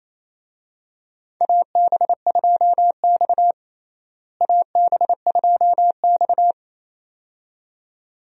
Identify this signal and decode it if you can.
Morse code, heard as a keyed tone, about 27 wpm, 700 Hz. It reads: AB2X AB2X
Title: USA Call Signs - 4 Characters - Repeated 2x - 28wpm